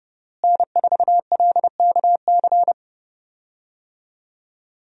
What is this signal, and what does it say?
Morse code, heard as a keyed tone, about 30 wpm, 700 Hz. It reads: N4LKC
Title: USA Call Signs - 5 Characters - Repeated 2x - 30wpm